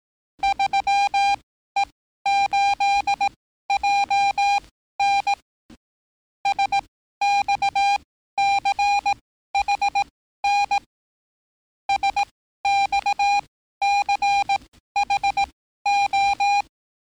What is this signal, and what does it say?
Morse code, heard as a keyed tone, about 18 wpm, spaced as for 13 wpm, 800 Hz. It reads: 3E8JN SXCHN SXCHO
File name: training / NAVPERS 11049to11060 / NAVPERS 11054 RA RB